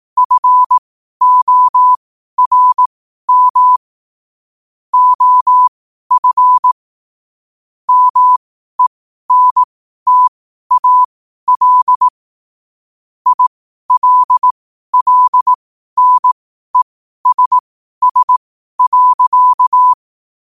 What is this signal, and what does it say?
Morse code, heard as a keyed tone, about 18 wpm, spaced as for 12 wpm, 1000 Hz. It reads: FORM OF MENTAL ILLNESS.
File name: qotdfiles / 12wpm